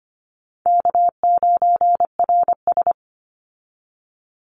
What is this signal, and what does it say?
Morse code, heard as a keyed tone, about 25 wpm, 700 Hz. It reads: K9RS
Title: Easy Call Signs - 25wpm